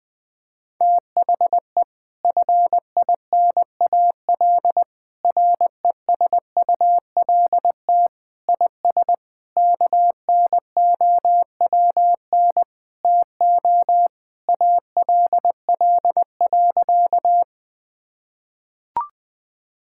Morse code, heard as a keyed tone, about 20 wpm, 700 Hz. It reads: THE FINAL RESULT IS KNOWN TO ALL.
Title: Sentences from Top 1000 Words - Review Part 5 - 20wpm